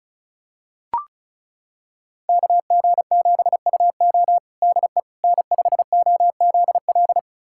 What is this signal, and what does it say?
Morse code, heard as a keyed tone, about 36 wpm, 700 Hz. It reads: KG7UO DE N5OZL
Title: USA Callsign DE Callsign - 5 Characters - 35wpm